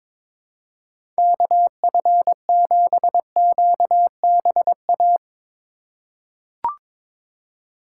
Morse code, heard as a keyed tone, about 22 wpm, 700 Hz. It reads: KF7QBA
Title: USA Call Signs - 6 Characters - 3x Character Spacing - 22wpm